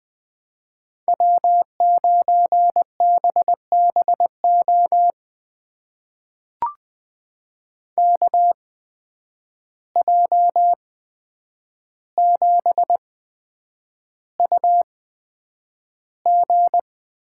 Morse code, heard as a keyed tone, about 20 wpm, spaced as for 9 wpm, 700 Hz. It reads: W9BBO KJ7UG
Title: USA Call Signs - 5 Characters - 8x Character Spacing - 20wpm